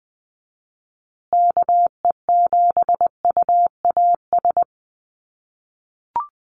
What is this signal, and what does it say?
Morse code, heard as a keyed tone, about 20 wpm, 700 Hz. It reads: KE7UAS
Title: USA Call Signs Repeated 3x 20wpm